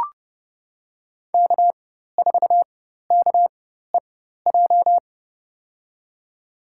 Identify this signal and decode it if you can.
Morse code, heard as a keyed tone, about 30 wpm, spaced as for 14 wpm, 700 Hz. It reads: K4KEJ
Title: USA Call Signs - 4x Character Spacing - 30wpm